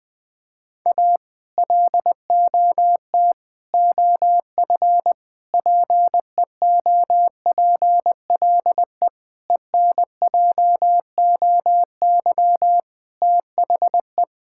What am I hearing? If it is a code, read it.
Morse code, heard as a keyed tone, about 21 wpm, 700 Hz. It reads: A LOT OF PEOPLE ENJOY THE
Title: Sentences with 4-Word Phrases - Spoken Beginning - 20wpm